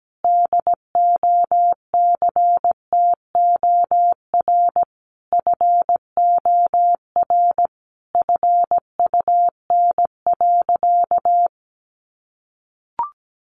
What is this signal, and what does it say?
Morse code, heard as a keyed tone, about 17 wpm, 700 Hz. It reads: DOCTOR FOR FUN.
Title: Additional Easy Phrases 17wpm